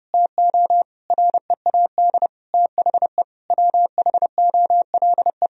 Morse code, heard as a keyed tone, about 30 wpm, 700 Hz. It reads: TO READ THE WHOLE
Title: Sentences from Top 1000 Words - Review Part 1 - 30wpm